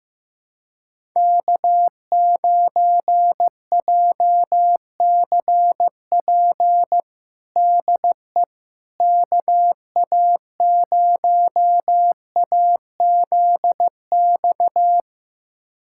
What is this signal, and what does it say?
Morse code, heard as a keyed tone, about 15 wpm, 700 Hz. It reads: K9JCP DE KA0AZX
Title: USA Callsign DE Callsign 15wpm